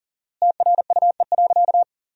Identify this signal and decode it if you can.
Morse code, heard as a keyed tone, about 37 wpm, 700 Hz. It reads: TRUE.
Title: 4x Word-Spacing - Sentences from Top 400 Words - 40wpm